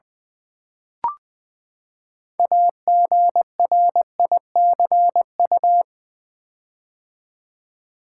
Morse code, heard as a keyed tone, about 20 wpm, 700 Hz. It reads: AGRICU
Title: Build A Word - 6 Letters At A Time - 12-Letter Words - 20wpm